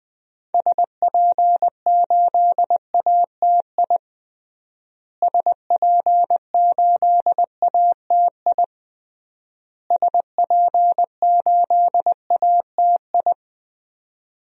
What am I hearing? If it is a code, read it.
Morse code, heard as a keyed tone, about 20 wpm, 700 Hz. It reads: SP8ATI SP8ATI SP8ATI
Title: DXCC-100 Call Signs - 6 Characters - Repeated 3x - 20wpm